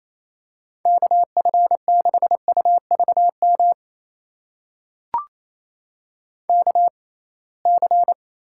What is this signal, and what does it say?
Morse code, heard as a keyed tone, about 28 wpm, spaced as for 21 wpm, 700 Hz. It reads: KF6UVM KC
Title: USA Call Signs - 6 Characters - 6x Character Spacing - 28wpm